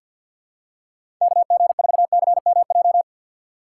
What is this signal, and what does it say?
Morse code, heard as a keyed tone, about 52 wpm, 700 Hz. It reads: KG4CMJ
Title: USA Call Signs - 6 Characters - 3x Character Spacing - 50wpm